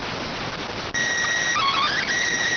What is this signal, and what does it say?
SSTV (slow-scan television), PD 290.